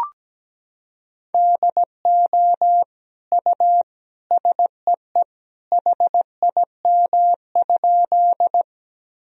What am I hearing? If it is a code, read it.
Morse code, heard as a keyed tone, about 17 wpm, 700 Hz. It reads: DO U SEE HIM?
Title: Sentences from Top 100 Words and Abbreviations 17wpm